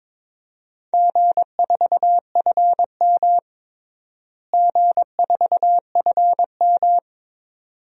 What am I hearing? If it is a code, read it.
Morse code, heard as a keyed tone, about 22 wpm, 700 Hz. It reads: G4FM G4FM
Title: DXCC-100 Call Signs - 4 Characters - Repeated 2x - 22wpm